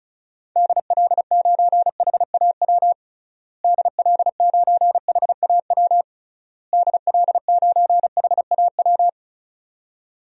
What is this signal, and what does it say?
Morse code, heard as a keyed tone, about 36 wpm, 700 Hz. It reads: DL9HAW DL9HAW DL9HAW
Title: DXCC-100 Call Signs - 6 Characters - Repeated 3x - 35wpm